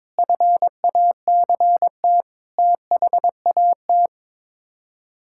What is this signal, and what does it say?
Morse code, heard as a keyed tone, about 22 wpm, 700 Hz. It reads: FACT THAT
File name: practice / 22wpm / Sets of 7 Words - 22wpm